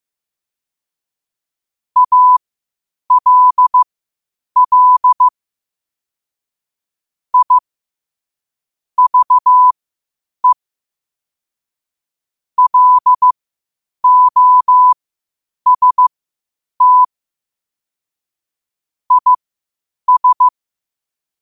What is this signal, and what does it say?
Morse code, heard as a keyed tone, about 15 wpm, spaced as for 8 wpm, 1000 Hz. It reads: ALL I VE LOST IS
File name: qotdfiles / 8wpm